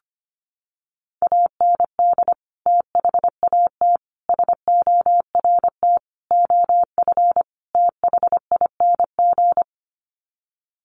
Morse code, heard as a keyed tone, about 25 wpm, 700 Hz. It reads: AND THAT SORT OF THING
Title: Sets of 5 Words Encore2 25wpm